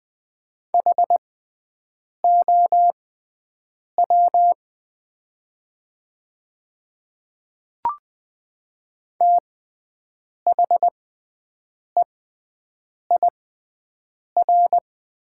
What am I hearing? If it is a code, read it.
Morse code, heard as a keyed tone, about 20 wpm, spaced as for 7 wpm, 700 Hz. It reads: HOW THEIR
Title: 6x Character Spacing - Top 100 Words - 20wpm